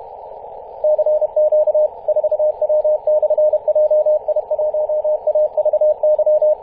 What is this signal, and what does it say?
Morse code, heard as a keyed tone, about 31 wpm, 610 Hz, with radio noise. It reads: CQ 4W/JI1AVY
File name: 2016/Apr/08 0544z 4W/JI1AVY 21.011MHz CW